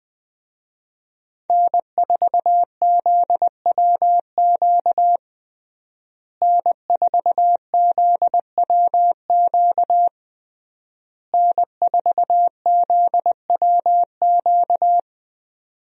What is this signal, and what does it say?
Morse code, heard as a keyed tone, about 20 wpm, 700 Hz. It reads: N4ZWQ N4ZWQ N4ZWQ
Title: USA Call Signs - 5 Characters - Repeated 3x - 20wpm